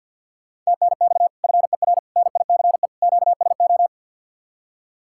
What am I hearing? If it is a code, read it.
Morse code, heard as a keyed tone, about 60 wpm, 700 Hz. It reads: TNX FER NICE QSO
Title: Speed-Racing Common QSO Phrases - 50wpm